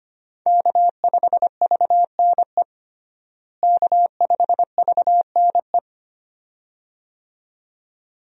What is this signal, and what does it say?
Morse code, heard as a keyed tone, about 25 wpm, 700 Hz. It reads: K5VNE K5VNE
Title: USA Call Signs - 5 Characters - Repeated 2x - 25wpm